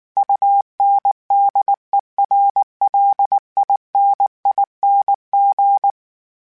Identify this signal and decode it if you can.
Morse code, heard as a keyed tone, about 19 wpm, 800 Hz. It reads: UNDERLINING